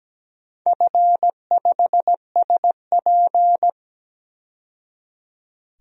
Morse code, heard as a keyed tone, about 17 wpm, 700 Hz. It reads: F5SP
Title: DXCC-100 Call Signs - 4 Characters - 17wpm